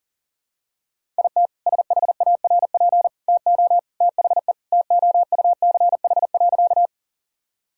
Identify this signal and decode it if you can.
Morse code, heard as a keyed tone, about 39 wpm, 700 Hz. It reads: IT SHARP TO THE TOUCH.
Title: Sentences from Top 1600 Words 40wpm